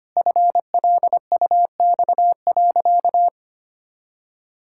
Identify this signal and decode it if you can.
Morse code, heard as a keyed tone, about 25 wpm, 700 Hz. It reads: FLUX.
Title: Pangram Challenge v2 - 25wpm